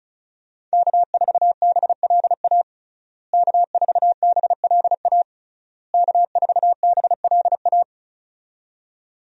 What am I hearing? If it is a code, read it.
Morse code, heard as a keyed tone, about 36 wpm, 700 Hz. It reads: K4BLA K4BLA K4BLA
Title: USA Call Signs - 5 Characters - Repeated 3x - 35wpm